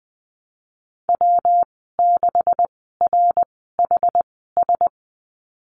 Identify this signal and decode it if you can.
Morse code, heard as a keyed tone, about 20 wpm, spaced as for 15 wpm, 700 Hz. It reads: W6RHS
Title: 179 Lesson 29g - Slash - Call Signs 20wpm